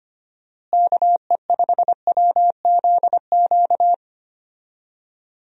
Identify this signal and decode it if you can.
Morse code, heard as a keyed tone, about 25 wpm, 700 Hz. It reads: KE5WZQ
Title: USA Call Signs - 6 Characters - Repeated 2x - 25wpm